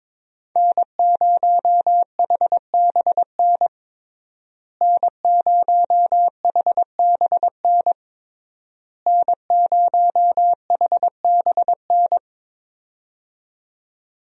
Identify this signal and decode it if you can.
Morse code, heard as a keyed tone, about 22 wpm, 700 Hz. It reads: N0HBN N0HBN N0HBN
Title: USA Call Signs - 5 Characters - Repeated 3x - 22wpm